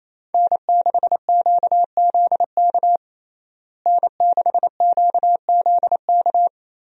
Morse code, heard as a keyed tone, about 27 wpm, 700 Hz. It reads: N6QZK N6QZK
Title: USA Call Signs - 5 Characters - Repeated 3x - 28wpm